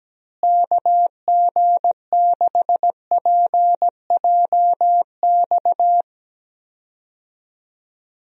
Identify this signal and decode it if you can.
Morse code, heard as a keyed tone, about 17 wpm, 700 Hz. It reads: KG6PJX
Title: USA Call Signs - 6 Characters - Repeated 2x - 17wpm